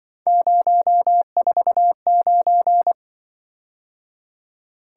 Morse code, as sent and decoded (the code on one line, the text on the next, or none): ----- ....- ----.
049